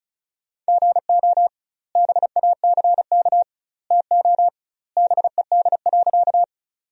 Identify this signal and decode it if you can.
Morse code, heard as a keyed tone, about 36 wpm, 700 Hz. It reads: GO BACK TO BED.
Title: 2x Word-Spacing - Sentences from Top 500 Words - 35wpm